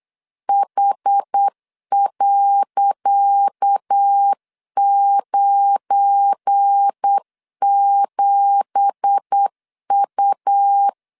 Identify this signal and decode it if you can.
Morse code, heard as a keyed tone, about 8 wpm, 800 Hz. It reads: H.97U
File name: LEZIONI / 10 SIGNS / 10C DOT 8wpm 3649